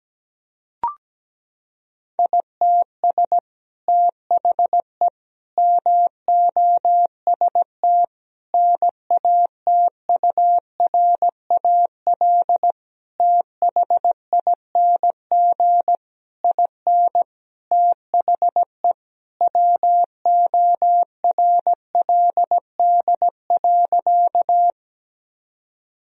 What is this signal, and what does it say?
Morse code, heard as a keyed tone, about 17 wpm, 700 Hz. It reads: ITS THE MOST NATURAL THING IN THE WORLD.
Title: Sentences from Top 1000 Words - Review Part 4 - 17wpm